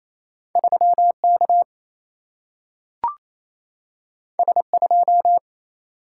Morse code, heard as a keyed tone, about 28 wpm, 700 Hz. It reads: 3K S2
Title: Single-Letter Single-Number 28wpm